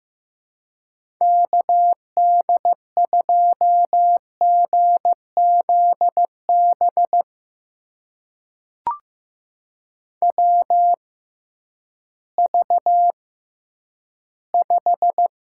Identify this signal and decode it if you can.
Morse code, heard as a keyed tone, about 15 wpm, spaced as for 10 wpm, 700 Hz. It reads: KD2GZB WV5
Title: USA Call Signs - 6x Character Spacing - 15wpm